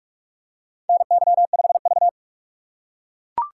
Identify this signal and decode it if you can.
Morse code, heard as a keyed tone, about 41 wpm, 700 Hz. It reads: NY5V
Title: USA Call Signs - 4 Characters - 3x Character Spacing - 45wpm